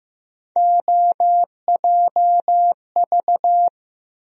Morse code, heard as a keyed tone, about 15 wpm, 700 Hz. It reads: OJV
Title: USA Call Sign Postfixes - 3 Letters 15wpm